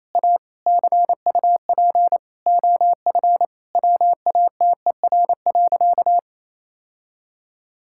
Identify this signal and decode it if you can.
Morse code, heard as a keyed tone, about 28 wpm, 700 Hz. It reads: A CUP OF WATER.
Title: Sentences from Top 1600 Words 28wpm